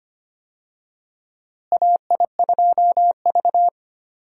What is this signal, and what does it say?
Morse code, heard as a keyed tone, about 25 wpm, 700 Hz. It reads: AI2V
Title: USA Call Signs - 4 Characters - Repeated 3x - 25wpm